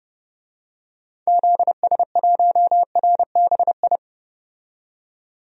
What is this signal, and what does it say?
Morse code, heard as a keyed tone, about 30 wpm, 700 Hz. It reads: ZS1RBI
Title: DXCC-100 Call Signs - 6 Characters - 30wpm